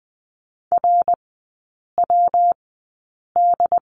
Morse code, heard as a keyed tone, about 20 wpm, spaced as for 8 wpm, 700 Hz. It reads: RWD
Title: 138 Lesson 23e - Letter f - Three Characters 20wpm